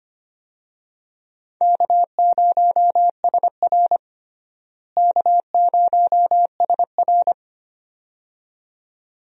Morse code, heard as a keyed tone, about 25 wpm, 700 Hz. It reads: K0SR K0SR
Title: USA Call Signs - 4 Characters - Repeated 2x - 25wpm